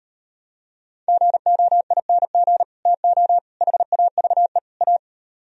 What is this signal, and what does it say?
Morse code, heard as a keyed tone, about 37 wpm, 700 Hz. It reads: GOING TO HAVE A